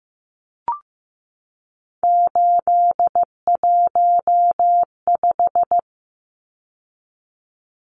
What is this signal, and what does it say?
Morse code, heard as a keyed tone, about 15 wpm, 700 Hz. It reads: 815
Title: Numbers - 3-Digit 15wpm